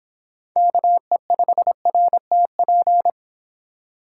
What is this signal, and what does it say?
Morse code, heard as a keyed tone, about 26 wpm, 700 Hz. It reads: KE5RTP